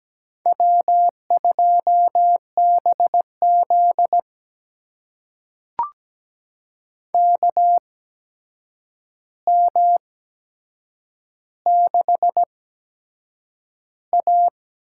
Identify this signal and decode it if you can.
Morse code, heard as a keyed tone, about 17 wpm, spaced as for 7 wpm, 700 Hz. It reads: W2BZ KM6A
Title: USA Call Signs - 4 Characters - 8x Character Spacing - 17wpm